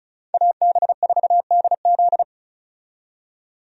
Morse code, heard as a keyed tone, about 34 wpm, 700 Hz. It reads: AB4DZ
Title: USA Call Signs - 5 Characters - 35wpm